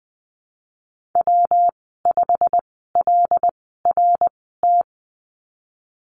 Morse code, heard as a keyed tone, about 19 wpm, spaced as for 14 wpm, 700 Hz. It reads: W5LRT
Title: 122 Lesson 20g - Number 3 - Call Signs 20wpm